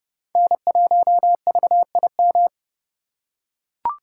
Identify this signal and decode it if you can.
Morse code, heard as a keyed tone, about 32 wpm, 700 Hz. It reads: N1VIM
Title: USA Call Signs - 5 Characters - 4x Character Spacing - 30wpm